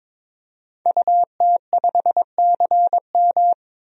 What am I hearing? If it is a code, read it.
Morse code, heard as a keyed tone, about 22 wpm, 700 Hz. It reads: UT5CM
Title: DXCC-100 Call Signs - 5 Characters - 22wpm